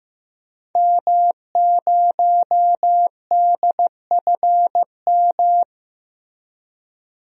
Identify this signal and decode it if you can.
Morse code, heard as a keyed tone, about 15 wpm, 700 Hz. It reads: M0DFM
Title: DXCC-100 Call Signs - 5 Characters - Repeated 3x - 15wpm